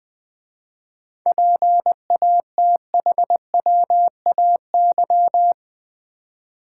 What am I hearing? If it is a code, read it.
Morse code, heard as a keyed tone, about 20 wpm, 700 Hz. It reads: PATHWAY
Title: 7 Letter Words - First 3 Letters Given - 20wpm